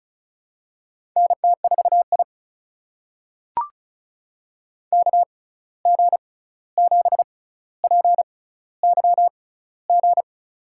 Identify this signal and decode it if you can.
Morse code, heard as a keyed tone, about 33 wpm, spaced as for 16 wpm, 700 Hz. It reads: NT4I KG7PYG
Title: USA Call Signs - 6x Character Spacing - 35wpm